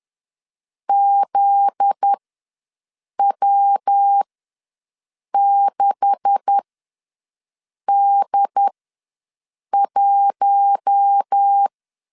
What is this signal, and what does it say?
Morse code, heard as a keyed tone, about 10 wpm, spaced as for 6 wpm, 800 Hz. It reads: ZW6D1
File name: LEZIONI / 9 286 ws 10wpm 4032